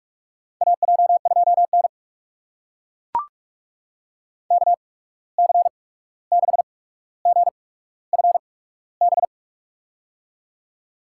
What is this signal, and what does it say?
Morse code, heard as a keyed tone, about 41 wpm, spaced as for 16 wpm, 700 Hz. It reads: AJ2N KC6GFB
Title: USA Call Signs - 8x Character Spacing - 45wpm